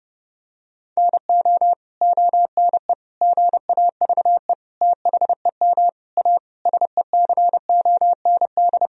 Morse code, heard as a keyed tone, about 30 wpm, 700 Hz. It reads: NO ONE GAVE THEM A SECOND
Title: Sentences from Top 1000 Words - Review Part 1 - 30wpm